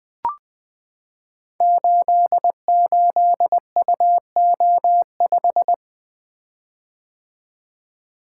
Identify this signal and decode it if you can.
Morse code, heard as a keyed tone, about 20 wpm, 700 Hz. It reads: 88UO5
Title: Code Groups - 5 Letter:Numbers - 20wpm